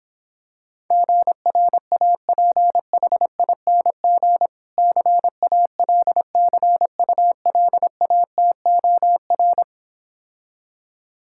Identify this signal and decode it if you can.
Morse code, heard as a keyed tone, about 26 wpm, 700 Hz. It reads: GRAPHING CALCULATOR